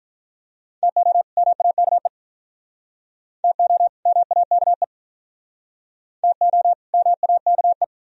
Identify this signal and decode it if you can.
Morse code, heard as a keyed tone, about 47 wpm, 700 Hz. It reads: TO MAKE TO MAKE TO MAKE